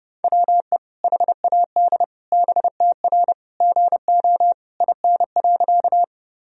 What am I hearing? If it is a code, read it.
Morse code, heard as a keyed tone, about 30 wpm, 700 Hz. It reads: WE HAD BTR GO IN.
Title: Sentences from Top 400 Words and Abbreviations 30wpm